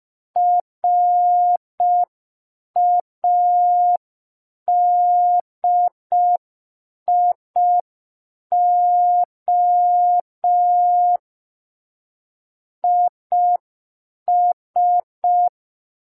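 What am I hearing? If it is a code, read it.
Morse code, heard as a keyed tone, about 5 wpm, 710 Hz. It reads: RADIO IS
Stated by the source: Morse receiving practice lesson for an amateur radio Foundation assessment.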